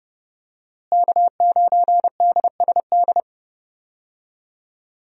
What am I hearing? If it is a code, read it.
Morse code, heard as a keyed tone, about 32 wpm, 700 Hz. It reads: K9DSD
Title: USA Call Signs - 5 Characters - Repeated 3x - 30wpm